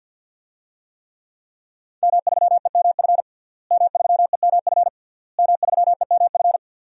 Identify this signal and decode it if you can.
Morse code, heard as a keyed tone, about 49 wpm, 700 Hz. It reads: M3EMF M3EMF M3EMF
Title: DXCC-100 Call Signs - 5 Characters - Repeated 3x - 50wpm